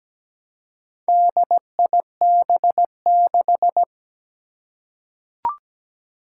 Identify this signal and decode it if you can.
Morse code, heard as a keyed tone, about 17 wpm, 700 Hz. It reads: DIB6
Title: Four Challenging Letter-Number Combinations 17wpm